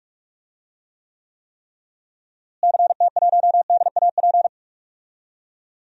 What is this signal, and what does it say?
Morse code, heard as a keyed tone, about 42 wpm, 700 Hz. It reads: CT1DAP
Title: DXCC-100 Call Signs - 6 Characters - Repeated 3x - 45wpm